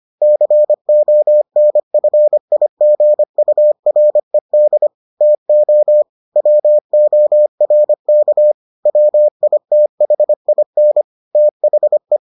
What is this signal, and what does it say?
Morse code, heard as a keyed tone, about 25 wpm, 600 Hz. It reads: CONFIGURED TO WORK WITHIN THE